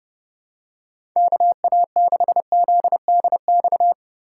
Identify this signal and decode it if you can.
Morse code, heard as a keyed tone, about 30 wpm, 700 Hz. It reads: KA6ZDX
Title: USA Call Signs - 2x Character Spacing - 30wpm